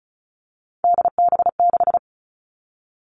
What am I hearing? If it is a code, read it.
Morse code, heard as a keyed tone, about 36 wpm, 700 Hz. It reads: DB6
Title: Three Challenging Letter-Number Combinations 35wpm